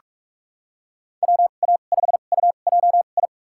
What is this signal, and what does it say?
Morse code, heard as a keyed tone, about 46 wpm, spaced as for 32 wpm, 700 Hz. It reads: WA5UJI
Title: USA Call Signs - 6 Characters - 2x Character Spacing - 45wpm